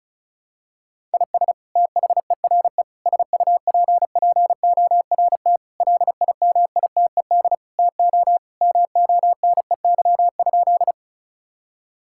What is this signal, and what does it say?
Morse code, heard as a keyed tone, about 36 wpm, 700 Hz. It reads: IS THERE SUPPORT LIMITED TO MONEY?
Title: Sentences from Top 1000 Words - Review Part 6 - 35wpm